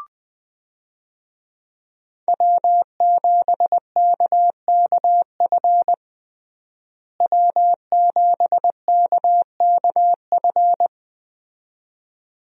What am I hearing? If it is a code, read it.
Morse code, heard as a keyed tone, about 20 wpm, 700 Hz. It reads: W7KKF W7KKF
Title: USA Call Signs - 5 Characters - Repeated 2x - 20wpm